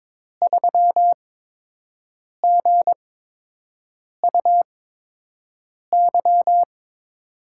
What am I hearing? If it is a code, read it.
Morse code, heard as a keyed tone, about 22 wpm, spaced as for 6 wpm, 700 Hz. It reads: 3GUY
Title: USA Call Signs - 5 Characters - 8x Character Spacing - 22wpm